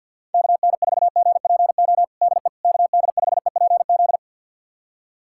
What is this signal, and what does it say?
Morse code, heard as a keyed tone, about 51 wpm, 700 Hz. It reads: KN4GPO DE KD5EPZ
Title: USA Callsign DE Callsign - 6 Characters - 50wpm